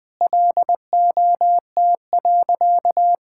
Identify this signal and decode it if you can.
Morse code, heard as a keyed tone, about 20 wpm, 700 Hz. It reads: LOT.
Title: Sentences with 3-Word Phrases - Spoken Beginning - 20wpm